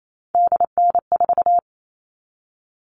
Morse code, heard as a keyed tone, about 28 wpm, 700 Hz. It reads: DN4
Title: All Cut Numbers - 3-Digit 28wpm